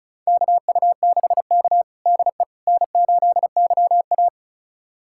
Morse code, heard as a keyed tone, about 36 wpm, 700 Hz. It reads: KU6K DE N8YA
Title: USA Callsign DE Callsign - 4 Characters - 35wpm